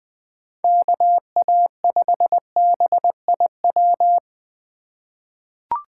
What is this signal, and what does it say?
Morse code, heard as a keyed tone, about 20 wpm, 700 Hz. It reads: KA5BIW
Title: USA Call Signs - 6 Characters - 3x Character Spacing - 20wpm